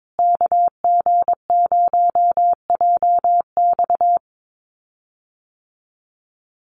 Morse code, heard as a keyed tone, about 22 wpm, 700 Hz. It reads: KG0JX
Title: USA Call Signs Repeated 3x 22wpm